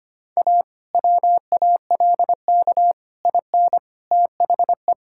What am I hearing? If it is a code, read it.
Morse code, heard as a keyed tone, about 25 wpm, 700 Hz. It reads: A WALK IN THE
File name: practice / 25wpm / Difficult Sentences with Hints - 25wpm